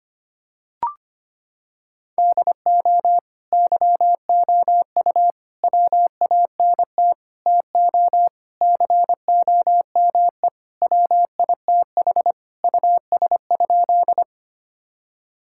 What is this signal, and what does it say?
Morse code, heard as a keyed tone, about 25 wpm, 700 Hz. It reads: DO YOU WANT TO COME WITH US?
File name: practice / 25wpm / Sentences with 3-Word Phrases - 25wpm